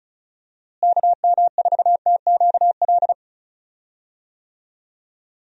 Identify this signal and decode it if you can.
Morse code, heard as a keyed tone, about 36 wpm, 700 Hz. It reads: KM4TQL
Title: USA Call Signs - 6 Characters - Repeated 3x - 35wpm